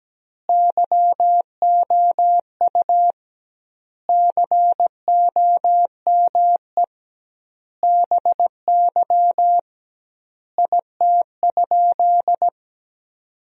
Morse code, heard as a keyed tone, about 17 wpm, 700 Hz. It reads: YOU COME BY IT?
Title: 2x Word-Spacing - Sentences from Top 100 Words - 17wpm